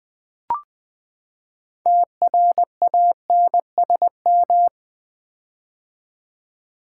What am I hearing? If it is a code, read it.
Morse code, heard as a keyed tone, about 20 wpm, 700 Hz. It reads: TRANSM
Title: Build A Word - 6 Letters At A Time - 12-Letter Words - 20wpm